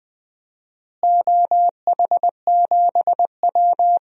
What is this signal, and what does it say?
Morse code, heard as a keyed tone, about 20 wpm, 700 Hz. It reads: OH7W
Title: DXCC-100 Call Signs - 4 Characters - Repeated 2x - 20wpm